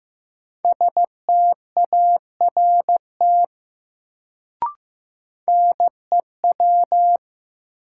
Morse code, heard as a keyed tone, about 15 wpm, 700 Hz. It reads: START NEW
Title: Sentences from Top 300 Words - New Word Drill - 15wpm